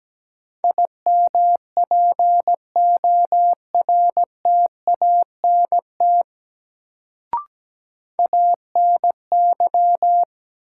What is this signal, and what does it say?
Morse code, heard as a keyed tone, about 17 wpm, 700 Hz. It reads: IMPORTANT ANY
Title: Sets of 3 Words - New Word Drill - 17wpm